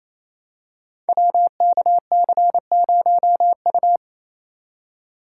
Morse code, heard as a keyed tone, about 29 wpm, 700 Hz. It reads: WKC0U
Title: Code Groups - 5 Letter:Numbers - 28wpm